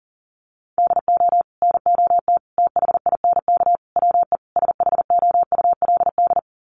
Morse code, heard as a keyed tone, about 40 wpm, 700 Hz. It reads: DO NOT THINK WE SHOULD
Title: Sets of 5 Words Encore 40wpm